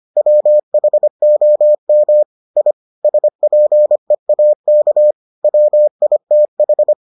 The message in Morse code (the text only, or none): WHOM I SPEAK WITH